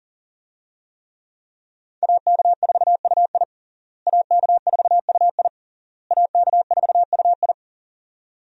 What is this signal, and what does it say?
Morse code, heard as a keyed tone, about 40 wpm, 700 Hz. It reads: AK4UI AK4UI AK4UI
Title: USA Call Signs - 5 Characters - Repeated 3x - 40wpm